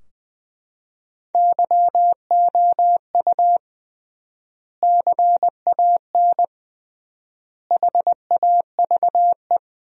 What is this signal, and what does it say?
Morse code, heard as a keyed tone, about 20 wpm, 700 Hz. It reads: YOU CAN HAVE
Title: 3x Word-Spacing - Sentences from Top 100 Words - 20wpm